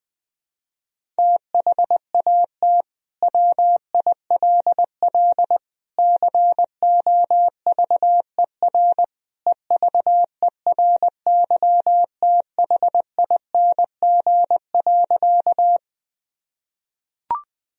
Morse code, heard as a keyed tone, about 21 wpm, 700 Hz. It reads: THAT WILL COVER EVERYTHING.
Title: Sentences from Top 1000 Words - Review Part 1 - 20wpm